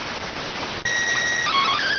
SSTV (slow-scan television), PD 290.